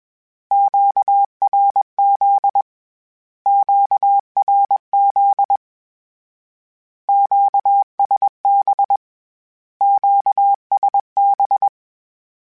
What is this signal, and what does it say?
Morse code, heard as a keyed tone, about 21 wpm, 800 Hz. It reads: QRZ QRZ QSB QSB
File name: Q Calls / Q Calls-mp3s / Q Calls Twice @20